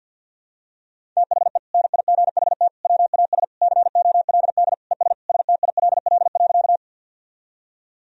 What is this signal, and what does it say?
Morse code, heard as a keyed tone, about 51 wpm, 700 Hz. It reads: THE NIGHT WAS COLD ES STILL.
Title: Sentences from Top 400 Words and Abbreviations 50wpm